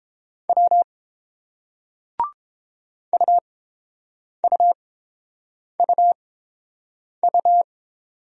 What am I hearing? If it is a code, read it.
Morse code, heard as a keyed tone, about 29 wpm, spaced as for 7 wpm, 700 Hz. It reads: W UUUU